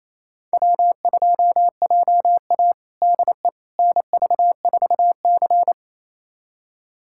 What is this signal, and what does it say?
Morse code, heard as a keyed tone, about 27 wpm, 700 Hz. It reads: W2JA DE NV4C
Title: USA Callsign DE Callsign - 4 Characters - 28wpm